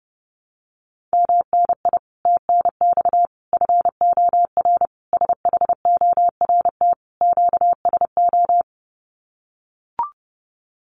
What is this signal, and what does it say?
Morse code, heard as a keyed tone, about 30 wpm, 700 Hz. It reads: MNI TNX FOR SHORT QSO
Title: Common QSO Phrases 30wpm